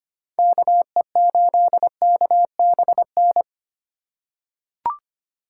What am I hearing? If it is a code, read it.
Morse code, heard as a keyed tone, about 25 wpm, 700 Hz. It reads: KE8KBN
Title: USA Call Signs - 2x Character Spacing - 25wpm